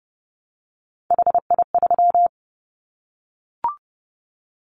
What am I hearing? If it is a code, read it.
Morse code, heard as a keyed tone, about 30 wpm, 700 Hz. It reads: HI3
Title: Callsign Prefixes for Top 100 DXCC - 30wpm